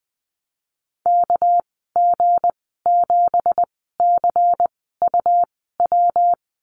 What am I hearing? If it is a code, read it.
Morse code, heard as a keyed tone, about 20 wpm, spaced as for 15 wpm, 700 Hz. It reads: KG7CUW
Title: 231 Lesson 37g - Letter x - Call Signs 20wpm